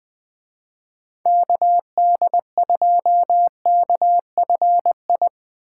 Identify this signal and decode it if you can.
Morse code, heard as a keyed tone, about 23 wpm, 700 Hz. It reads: KD2KFI
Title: Speed Racing - USA Call Signs 20wpm